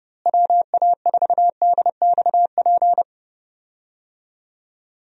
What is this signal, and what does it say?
Morse code, heard as a keyed tone, about 30 wpm, 700 Hz. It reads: WA4DXP
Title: USA Call Signs - 6 Characters - Repeated 3x - 30wpm